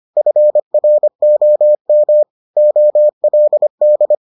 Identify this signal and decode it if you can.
Morse code, heard as a keyed tone, about 25 wpm, 600 Hz. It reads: FROM OLD